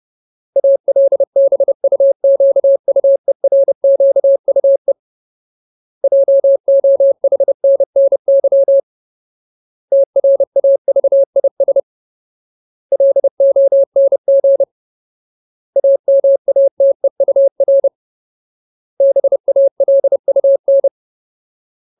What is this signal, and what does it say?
Morse code, heard as a keyed tone, about 30 wpm, 550 Hz. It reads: ALBUQUERQUE JOHNNY TRAVIS LONG AMATEUR BALUN
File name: Ultimate Word List / 30WPM Ultimate